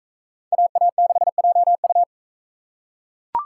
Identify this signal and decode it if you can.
Morse code, heard as a keyed tone, about 41 wpm, 700 Hz. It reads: AA6JU